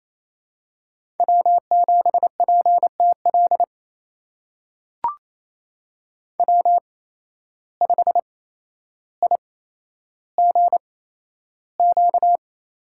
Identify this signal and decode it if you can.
Morse code, heard as a keyed tone, about 28 wpm, spaced as for 12 wpm, 700 Hz. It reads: W7PTL W5IGQ
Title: USA Call Signs - 8x Character Spacing - 28wpm